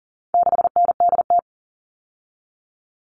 Morse code, heard as a keyed tone, about 40 wpm, 700 Hz. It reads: 6NDT
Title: All Cut Numbers - 4-Digit 40wpm